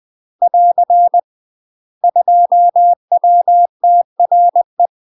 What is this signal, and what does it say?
Morse code, heard as a keyed tone, about 20 wpm, 700 Hz. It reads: <AR> 2WTRE
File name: UK MASTER QSOs TRAINING / 20WPM / UK MASTER QSOs 0070